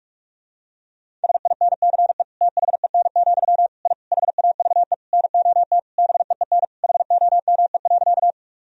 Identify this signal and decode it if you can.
Morse code, heard as a keyed tone, about 46 wpm, 700 Hz. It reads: SINCE THEN, I HAVE NOT BEEN HOME.
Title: Sentences from Top 1000 Words - Review Part 1 - 45wpm